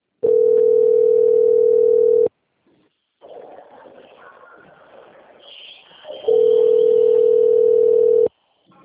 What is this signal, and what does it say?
Ringback tone (North America), 440 + 480 Hz.